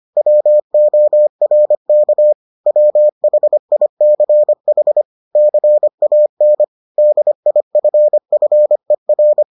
Morse code, heard as a keyed tone, about 25 wpm, 600 Hz. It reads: WORK WHICH CAN DIFFER